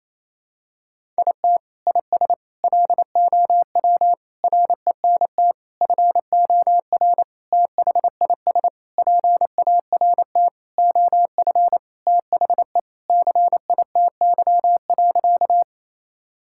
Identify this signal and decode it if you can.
Morse code, heard as a keyed tone, about 27 wpm, 700 Hz. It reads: IT IS LOW RENT FOR THIS PART OF THE CITY.
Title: Sentences from Top 1000 Words - Review Part 5 - 28wpm